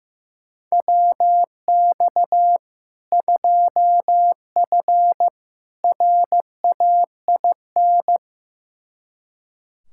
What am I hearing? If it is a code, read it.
Morse code, heard as a keyed tone, about 15 wpm, 700 Hz. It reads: WX 2F RAIN
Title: K1USN SST QSO Practice 15wpm